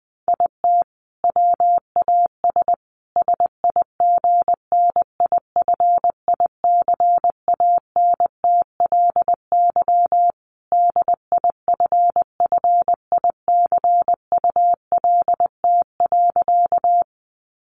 Morse code, heard as a keyed tone, about 20 wpm, 700 Hz. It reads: IT WAS SIGNIFICANTLY DIFFICULT.
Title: Sentences from Top 1500 Words 20wpm